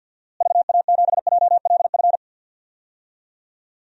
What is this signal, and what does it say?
Morse code, heard as a keyed tone, about 52 wpm, 700 Hz. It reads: VA7JLF
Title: DXCC-100 Call Signs - 6 Characters - Repeated 2x - 50wpm